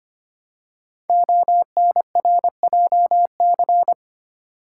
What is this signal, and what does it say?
Morse code, heard as a keyed tone, about 25 wpm, 700 Hz. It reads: ONRJC
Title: Code Groups - 5 Letters - 25wpm